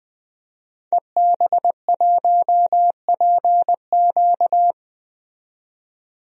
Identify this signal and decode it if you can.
Morse code, heard as a keyed tone, about 20 wpm, 700 Hz. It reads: EB1PQ
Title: DXCC-100 Call Signs - 5 Characters - 20wpm